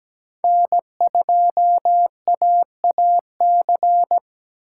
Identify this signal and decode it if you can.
Morse code, heard as a keyed tone, about 17 wpm, 700 Hz. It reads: N2AAC
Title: USA Call Signs - 5 Characters - 8x Character Spacing - 17wpm